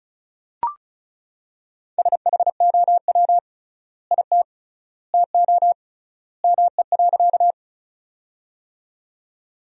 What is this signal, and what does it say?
Morse code, heard as a keyed tone, about 36 wpm, 700 Hz. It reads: SHOW IT TO ME.
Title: 3x Word-Spacing - Sentences from Top 200 Words - 35wpm